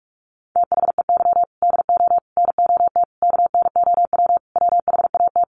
Morse code, heard as a keyed tone, about 44 wpm, 700 Hz. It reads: THEY DO NOT KNOW WHAT
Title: Sets of 5 Words Encore 45wpm